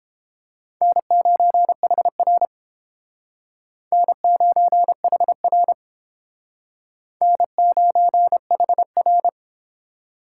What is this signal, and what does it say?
Morse code, heard as a keyed tone, about 29 wpm, 700 Hz. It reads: N9HR N9HR N9HR